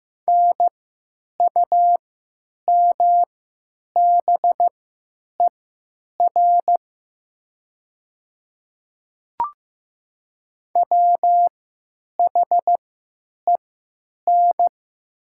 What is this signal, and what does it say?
Morse code, heard as a keyed tone, about 15 wpm, spaced as for 9 wpm, 690 Hz. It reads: NUMBER WHEN
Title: 3x Character Spacing - Top 100 Words - 15wpm